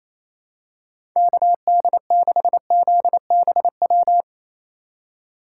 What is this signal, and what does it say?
Morse code, heard as a keyed tone, about 27 wpm, 700 Hz. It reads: KD6ZBW
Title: USA Call Signs - 4x Character Spacing - 28wpm